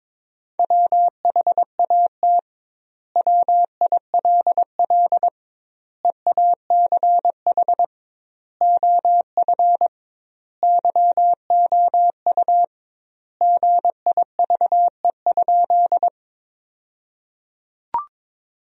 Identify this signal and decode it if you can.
Morse code, heard as a keyed tone, about 21 wpm, 700 Hz. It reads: WHAT WILL EACH OF YOU GIVE?
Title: 2x Word-Spacing - Sentences from Top 200 Words - 22wpm